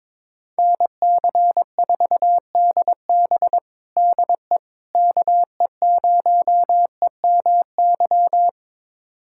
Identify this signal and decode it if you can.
Morse code, heard as a keyed tone, about 22 wpm, 700 Hz. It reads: NC4DB DE KE0EMY
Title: USA Callsign DE Callsign 22wpm